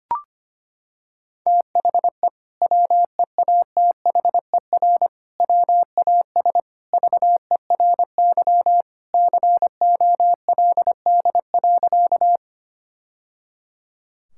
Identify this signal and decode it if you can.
Morse code, heard as a keyed tone, about 25 wpm, 700 Hz. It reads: THE WEATHER WAS VERY COLD.
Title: Sentences from Top 1000 Words - Review Part 5 - 25wpm